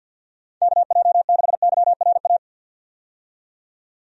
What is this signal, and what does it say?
Morse code, heard as a keyed tone, about 52 wpm, 700 Hz. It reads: KJ6YRA
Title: USA Call Signs - 6 Characters - 50wpm